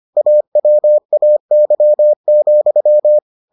Morse code, heard as a keyed tone, about 25 wpm, 600 Hz. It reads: AWAY,